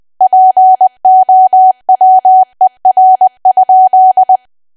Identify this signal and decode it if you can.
Morse code, heard as a keyed tone, about 20 wpm, 750 Hz. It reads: POWER?
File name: cwa / W1AW-Level3 / 20WPM